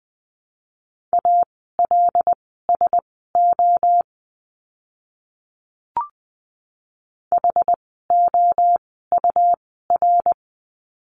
Morse code, heard as a keyed tone, about 20 wpm, spaced as for 14 wpm, 700 Hz. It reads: ALSO HOUR
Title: 108 Lesson 18f - Letter m - Words 20wpm